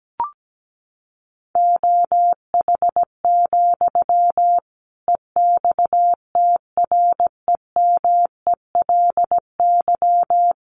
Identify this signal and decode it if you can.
Morse code, heard as a keyed tone, about 17 wpm, 700 Hz. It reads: OH, EXTREMELY
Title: Sentences from Top 1400 Words 17wpm